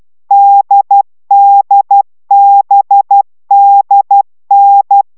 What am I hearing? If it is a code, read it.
Morse code, heard as a keyed tone, about 12 wpm, 800 Hz. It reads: DDBDN